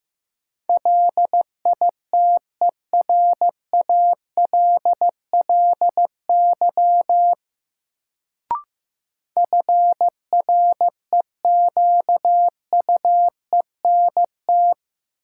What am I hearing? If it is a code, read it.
Morse code, heard as a keyed tone, about 15 wpm, 700 Hz. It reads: LITERALLY FREQUENT
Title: Sentences from Top 1400 Words - New Word Drill - 15wpm